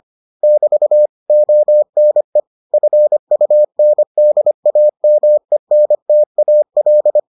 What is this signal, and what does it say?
Morse code, heard as a keyed tone, about 25 wpm, 600 Hz. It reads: <BT> ONE FUNDAMENTAL